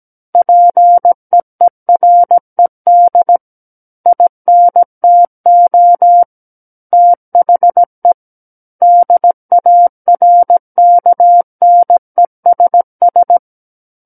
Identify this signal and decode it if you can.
Morse code, heard as a keyed tone, about 17 wpm, 700 Hz. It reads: PEERED INTO THE DARKNESS
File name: Book One / 08-Chapter 8 - 17 WPM